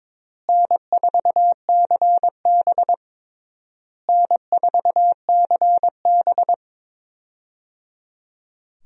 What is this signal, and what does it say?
Morse code, heard as a keyed tone, about 22 wpm, 700 Hz. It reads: N4CB N4CB
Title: USA Call Signs - 4 Characters - Repeated 3x - 22wpm